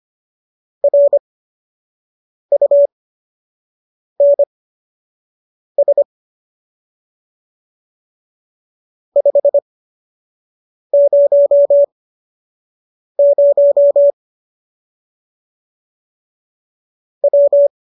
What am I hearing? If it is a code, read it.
Morse code, heard as a keyed tone, about 25 wpm, spaced as for 6 wpm, 570 Hz. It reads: RUNS 500 W
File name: qso002 06